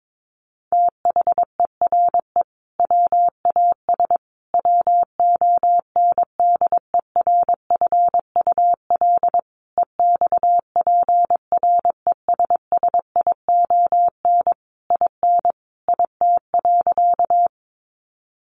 Morse code, heard as a keyed tone, about 22 wpm, 700 Hz. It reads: THERE WAS WONDERFUL EXPRESSION IN IT.
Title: Sentences from Top 1200 Words 22wpm